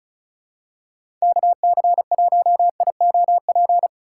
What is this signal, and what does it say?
Morse code, heard as a keyed tone, about 36 wpm, 700 Hz. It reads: KC1IOP
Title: USA Call Signs - 6 Characters - 3x Character Spacing - 35wpm